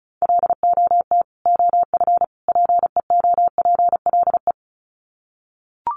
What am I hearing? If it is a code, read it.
Morse code, heard as a keyed tone, about 34 wpm, 700 Hz. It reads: LOT OF PEOPLE
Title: Sets of 3 Words 35wpm